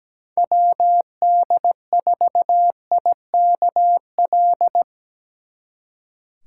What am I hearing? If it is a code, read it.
Morse code, heard as a keyed tone, about 20 wpm, 700 Hz. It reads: WD4IKL
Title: Speed Racing - USA 2x3 Call Signs 17wpm